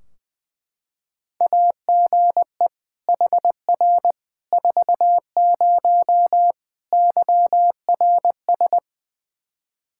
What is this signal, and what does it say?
Morse code, heard as a keyed tone, about 20 wpm, 700 Hz. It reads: AGE HR 40 YRS
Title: Practice QSO - Exchange 3 - More Info - 20wpm